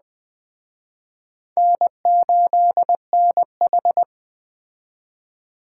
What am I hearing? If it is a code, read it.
Morse code, heard as a keyed tone, about 21 wpm, 700 Hz. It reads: N8NH